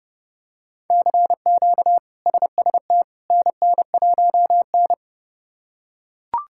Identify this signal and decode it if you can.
Morse code, heard as a keyed tone, about 31 wpm, 700 Hz. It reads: CQ SST NN1N
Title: K1USN SST QSO Practice - 30wpm